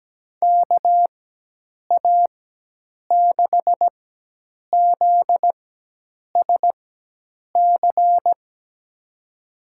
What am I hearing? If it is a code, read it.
Morse code, heard as a keyed tone, about 17 wpm, spaced as for 8 wpm, 700 Hz. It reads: KA6ZSC
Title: USA Call Signs - 4x Character Spacing - 17wpm